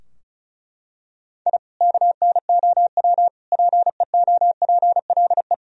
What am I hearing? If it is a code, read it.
Morse code, heard as a keyed tone, about 34 wpm, 700 Hz. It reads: I KNOW PEOPLE
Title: Sentences from Top 1000 Words - Review Part 4 - 35wpm